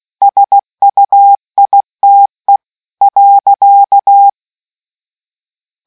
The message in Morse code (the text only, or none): SUITE .